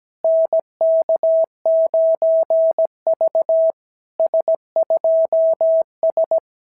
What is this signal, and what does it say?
Morse code, heard as a keyed tone, about 17 wpm, 660 Hz. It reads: NK9V S2S
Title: SOTA QSO Practice 17wpm